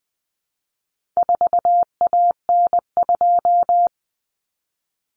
Morse code, heard as a keyed tone, about 20 wpm, 700 Hz. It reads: 4AN2
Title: Common Cut Numbers - 4-Digit 20wpm